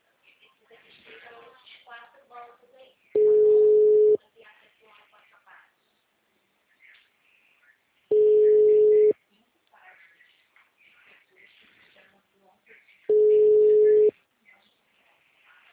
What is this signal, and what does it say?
Ringback tone (Europe, 425 Hz), with speech.